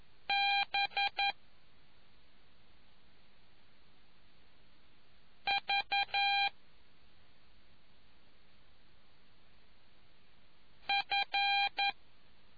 Morse code, heard as a keyed tone, about 11 wpm, 800 Hz. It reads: B V F